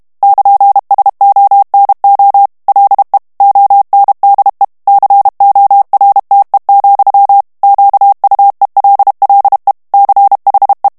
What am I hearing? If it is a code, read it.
Morse code, heard as a keyed tone, about 32 wpm, 800 Hz. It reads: <KN>SONO LE ONDE CORTE, QUELLE CHE
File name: ITALIEN texte clair 1 vitesse 30 mots minute